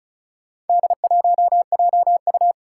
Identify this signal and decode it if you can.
Morse code, heard as a keyed tone, about 36 wpm, 700 Hz. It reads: D1JU
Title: Four Challenging Letter-Number Combinations 35wpm